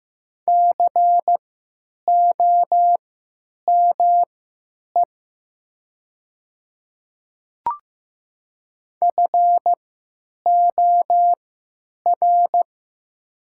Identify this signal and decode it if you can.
Morse code, heard as a keyed tone, about 15 wpm, spaced as for 9 wpm, 700 Hz. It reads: COME FOR
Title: 3x Character Spacing - Top 100 Words - 15wpm